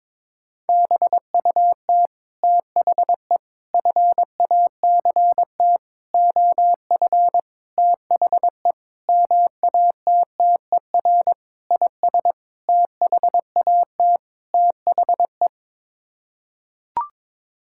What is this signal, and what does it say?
Morse code, heard as a keyed tone, about 22 wpm, 700 Hz. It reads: BUT THE FACT OF THE MATTER IS THAT THE
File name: Sets of 9 Words - 22wpm